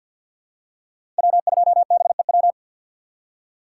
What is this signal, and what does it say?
Morse code, heard as a keyed tone, about 52 wpm, 700 Hz. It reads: W2BEW
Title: USA Call Signs - 5 Characters - 6x Character Spacing - 50wpm